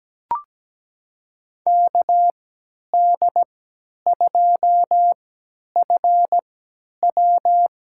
Morse code, heard as a keyed tone, about 17 wpm, spaced as for 10 wpm, 700 Hz. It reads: KD2FW
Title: USA Call Signs - 6 Characters - 3x Character Spacing - 17wpm